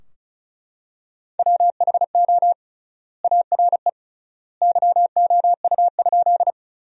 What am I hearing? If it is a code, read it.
Morse code, heard as a keyed tone, about 34 wpm, 700 Hz. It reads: WHO ARE YOU?
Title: 3x Word-Spacing - Sentences from Top 100 Words - 35wpm